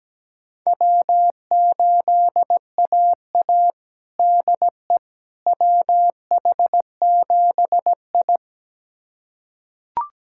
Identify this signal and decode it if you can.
Morse code, heard as a keyed tone, about 17 wpm, 700 Hz. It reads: W8AA DE WH7I
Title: USA Callsign DE Callsign - 4 Characters - 17wpm